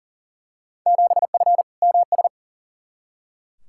Morse code, heard as a keyed tone, about 40 wpm, 700 Hz. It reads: 7F MS
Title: Field Day Exchanges Encore - 40wpm